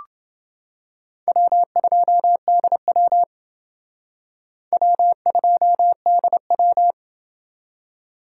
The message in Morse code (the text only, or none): W2DW W2DW